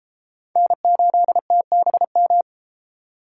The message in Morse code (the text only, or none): N8TBM